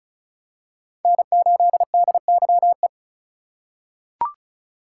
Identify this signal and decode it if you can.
Morse code, heard as a keyed tone, about 36 wpm, 700 Hz. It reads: N8DYE
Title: USA Call Signs - 5 Characters - 4x Character Spacing - 35wpm